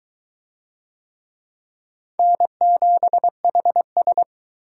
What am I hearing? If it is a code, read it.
Morse code, heard as a keyed tone, about 21 wpm, 700 Hz. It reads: N7HS